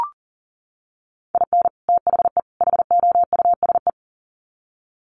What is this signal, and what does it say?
Morse code, heard as a keyed tone, about 40 wpm, 700 Hz. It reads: IN THE HOUSE
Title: Sets of 3 Words 40wpm